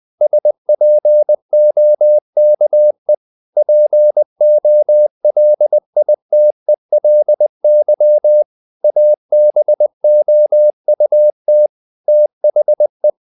Morse code, heard as a keyed tone, about 20 wpm, 600 Hz. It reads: SPOKE POLITELY ABOUT THE
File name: the old man and the sea / the old man and the sea part 1 20wpm